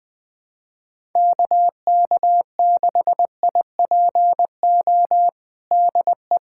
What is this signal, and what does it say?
Morse code, heard as a keyed tone, about 20 wpm, 700 Hz. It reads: KK6IPO DE
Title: USA Callsign DE Callsign 20wpm